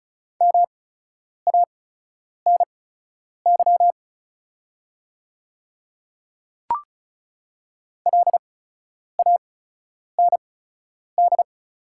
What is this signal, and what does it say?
Morse code, heard as a keyed tone, about 36 wpm, spaced as for 10 wpm, 690 Hz. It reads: MANY LAND
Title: 8x Character Spacing - Top 200 Words - 35wpm